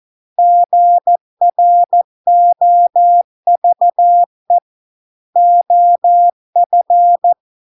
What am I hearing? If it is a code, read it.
Morse code, heard as a keyed tone, about 14 wpm, 700 Hz. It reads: GROVE OF